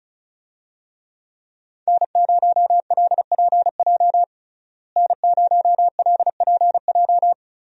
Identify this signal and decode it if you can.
Morse code, heard as a keyed tone, about 34 wpm, 700 Hz. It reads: N0LPJ N0LPJ
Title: USA Call Signs - 5 Characters - Repeated 2x - 35wpm